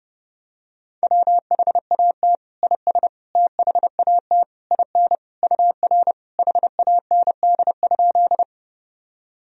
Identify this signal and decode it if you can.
Morse code, heard as a keyed tone, about 32 wpm, 700 Hz. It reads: WHAT IS THAT IN UR HAND?
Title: Sentences from Top 200 Words and Abbreviations 30wpm